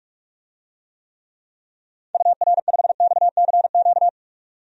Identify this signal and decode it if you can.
Morse code, heard as a keyed tone, about 42 wpm, 700 Hz. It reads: UR5XCQ
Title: DXCC-100 Call Signs - 6 Characters - 45wpm